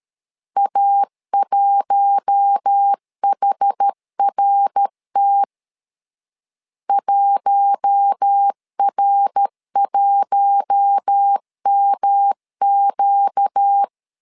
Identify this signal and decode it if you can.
Morse code, heard as a keyed tone, about 13 wpm, 800 Hz. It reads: A1HRT 1R1MQ
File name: LEZIONI / 4 Q1 12wpm 2028